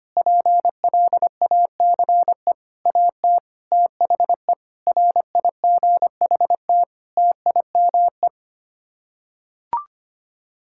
Morse code, heard as a keyed tone, about 25 wpm, 700 Hz. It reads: PLACE AT THE RIGHT TIME
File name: Sets of 9 Words - 25wpm